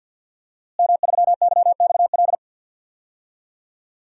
Morse code, heard as a keyed tone, about 49 wpm, 700 Hz. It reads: M3YXL
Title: DXCC-100 Call Signs - 5 Characters - Repeated 2x - 50wpm